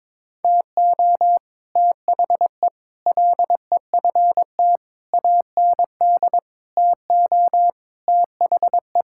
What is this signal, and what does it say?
Morse code, heard as a keyed tone, about 22 wpm, 700 Hz. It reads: TO THE LEFT AND TO THE
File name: practice / 22wpm / Sets of 8 Words - 22wpm